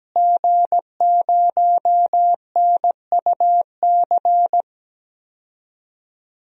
Morse code, heard as a keyed tone, about 17 wpm, 700 Hz. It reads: G0NUC
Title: DXCC-100 Call Signs - 5 Characters - Repeated 2x - 17wpm